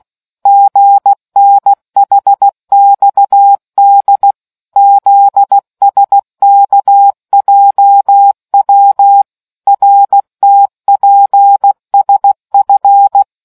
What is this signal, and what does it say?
Morse code, heard as a keyed tone, about 16 wpm, 800 Hz. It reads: GNHXD ZSKJW RTPSF